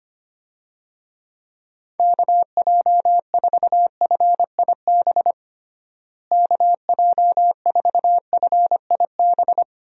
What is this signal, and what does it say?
Morse code, heard as a keyed tone, about 25 wpm, 700 Hz. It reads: KJ4FIB KJ4FIB
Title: USA Call Signs - 6 Characters - Repeated 3x - 25wpm